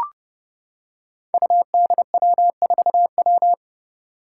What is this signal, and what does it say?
Morse code, heard as a keyed tone, about 30 wpm, 700 Hz. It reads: UDW4W
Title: Five Challenging Letter-Number Combinations 30wpm